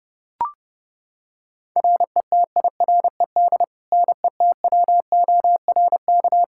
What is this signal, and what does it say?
Morse code, heard as a keyed tone, about 30 wpm, 700 Hz. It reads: RETIRED NETWORK
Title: Practice QSO - Exchange 3 - More Info - 30wpm